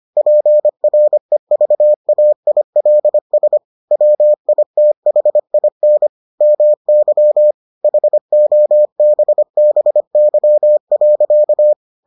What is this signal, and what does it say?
Morse code, heard as a keyed tone, about 25 wpm, 600 Hz. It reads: PREVAILS WITHIN MY HOBBY.